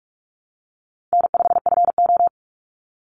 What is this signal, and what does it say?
Morse code, heard as a keyed tone, about 44 wpm, 700 Hz. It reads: N5FO
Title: Easy Call Signs Repeated 2x - 45wpm